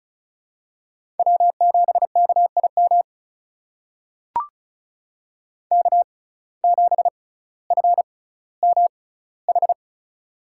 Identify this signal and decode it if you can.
Morse code, heard as a keyed tone, about 34 wpm, spaced as for 18 wpm, 700 Hz. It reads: W7KIM K7FMH
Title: USA Call Signs - 5 Characters - 6x Character Spacing - 35wpm